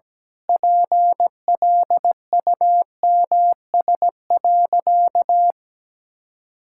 Morse code, heard as a keyed tone, about 17 wpm, 700 Hz. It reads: PLUMS.
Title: Pangram Challenge v2 - 17wpm